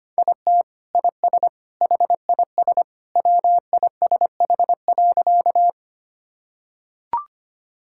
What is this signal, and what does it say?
Morse code, heard as a keyed tone, about 25 wpm, 700 Hz. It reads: IT IS HIS WISH.
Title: Sentences from Top 1600 Words 25wpm